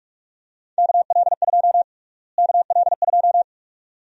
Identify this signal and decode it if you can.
Morse code, heard as a keyed tone, about 45 wpm, 700 Hz. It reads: KL2 KL2
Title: USA Call Sign Prefixes - Repeated 2x - 45wpm